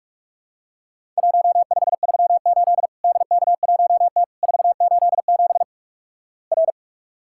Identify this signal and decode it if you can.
Morse code, heard as a keyed tone, about 48 wpm, 700 Hz. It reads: 1538 DK1T 487 R
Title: Worked All Europe DX Contest - QTC Exchange - 45wpm